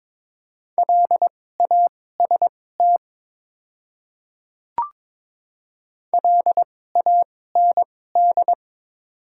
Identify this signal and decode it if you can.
Morse code, heard as a keyed tone, about 22 wpm, spaced as for 16 wpm, 700 Hz. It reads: LAST LAND
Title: 2x Character Spacing - Top 300 Words - 22wpm